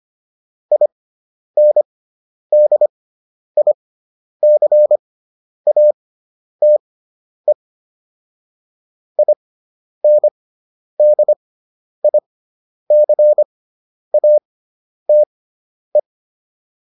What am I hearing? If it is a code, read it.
Morse code, heard as a keyed tone, about 25 wpm, spaced as for 10 wpm, 600 Hz. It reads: INDICATE INDICATE